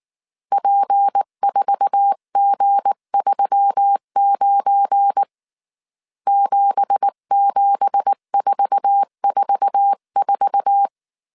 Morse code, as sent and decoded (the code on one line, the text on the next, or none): .--. ....- --. ...-- ----. / --... --... ....- ....- ....-
P4G39 77444